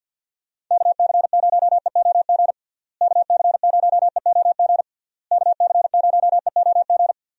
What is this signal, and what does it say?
Morse code, heard as a keyed tone, about 47 wpm, 700 Hz. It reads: KC0EOG KC0EOG KC0EOG
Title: USA Call Signs - 6 Characters - Repeated 3x - 50wpm